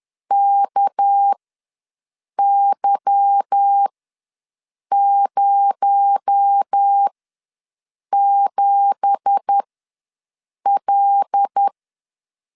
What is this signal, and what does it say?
Morse code, heard as a keyed tone, about 10 wpm, spaced as for 6 wpm, 800 Hz. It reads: KY07L
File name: LEZIONI / 10 SIGNS / 10A SLASH ws 10wpm 4121